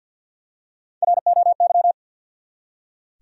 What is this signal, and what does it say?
Morse code, heard as a keyed tone, about 46 wpm, 700 Hz. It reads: ROY
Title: USA Common Names - Max 3 Letters - 50wpm